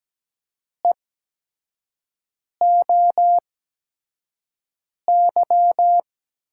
Morse code, heard as a keyed tone, about 17 wpm, spaced as for 5 wpm, 700 Hz. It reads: EOY
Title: USA Call Signs - 8x Character Spacing - 17wpm